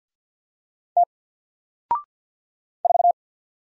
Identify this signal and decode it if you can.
Morse code, heard as a keyed tone, about 50 wpm, 700 Hz. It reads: T 4
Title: Single Letter-Number Alt 50wpm